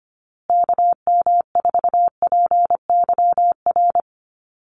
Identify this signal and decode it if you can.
Morse code, heard as a keyed tone, about 25 wpm, 700 Hz. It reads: KM4PYR
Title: USA Call Signs Repeated 2x 25wpm